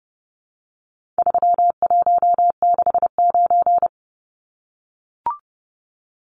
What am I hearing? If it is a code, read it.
Morse code, heard as a keyed tone, about 30 wpm, 700 Hz. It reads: 3169
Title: Numbers - 4-Digit 30wpm